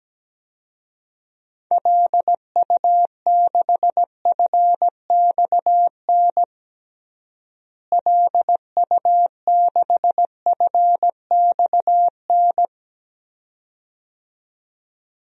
Morse code, heard as a keyed tone, about 17 wpm, 700 Hz. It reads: LU6FXN LU6FXN
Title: DXCC-100 Call Signs - 6 Characters - Repeated 2x - 17wpm